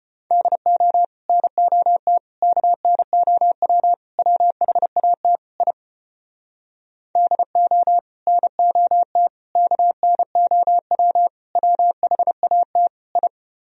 Morse code, heard as a keyed tone, about 33 wpm, 700 Hz. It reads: DO NOT KNOW WHAT I DO NOT KNOW WHAT I